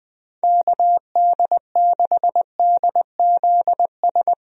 Morse code, heard as a keyed tone, about 20 wpm, 700 Hz. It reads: KD6DZS
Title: USA Call Signs - 6 Characters - 20wpm